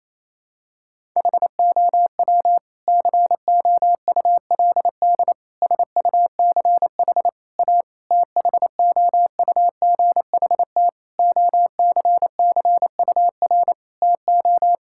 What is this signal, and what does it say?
Morse code, heard as a keyed tone, about 28 wpm, 700 Hz. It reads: HOW COULD SUCH A THOUGHT OCCUR TO
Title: Sentences from Top 1000 Words - Review Part 6 - 28wpm